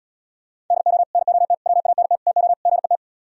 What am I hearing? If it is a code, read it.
Morse code, heard as a keyed tone, about 77 wpm, 700 Hz. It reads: DO NOT WANT TO GET